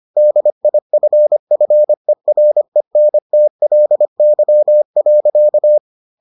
Morse code, heard as a keyed tone, about 25 wpm, 600 Hz. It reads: DIFFERENTLY.